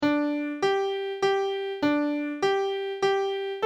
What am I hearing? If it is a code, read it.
There is no code.